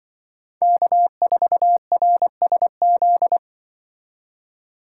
Morse code, heard as a keyed tone, about 25 wpm, 700 Hz. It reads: K4RSZ
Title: Speed Racing - USA 1x3 Call Signs 20wpm